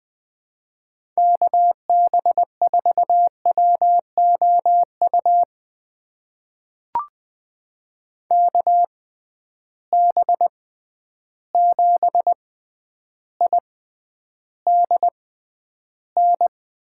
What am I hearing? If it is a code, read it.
Morse code, heard as a keyed tone, about 20 wpm, spaced as for 10 wpm, 700 Hz. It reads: KB4WOU KB7IDN
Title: USA Call Signs - 6 Characters - 6x Character Spacing - 20wpm